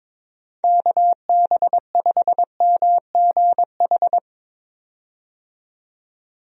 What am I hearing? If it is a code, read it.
Morse code, heard as a keyed tone, about 21 wpm, 700 Hz. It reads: KB5MGH
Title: USA Call Signs - 6 Characters - 22wpm